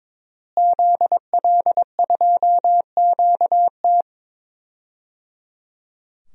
Morse code, heard as a keyed tone, about 23 wpm, 700 Hz. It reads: ZL2QT
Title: DXCC-100 Call Signs - 5 Characters - Repeated 3x - 22wpm